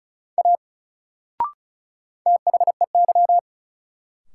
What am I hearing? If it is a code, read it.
Morse code, heard as a keyed tone, about 36 wpm, 700 Hz. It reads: A THEY
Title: Sentences from Top 100 Words - New Word Drill - 35wpm